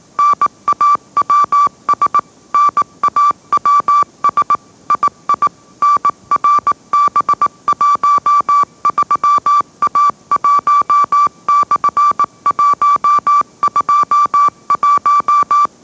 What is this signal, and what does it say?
Morse code, heard as a keyed tone, about 18 wpm, 1160 Hz. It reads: NAWS NAWS II NRB13A1/121
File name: mrx / mrx081 20wpm